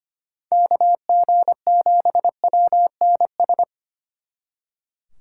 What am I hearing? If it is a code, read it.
Morse code, heard as a keyed tone, about 29 wpm, 700 Hz. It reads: KG7WNS
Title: Speed Racing - USA 2x3 Call Signs 25wpm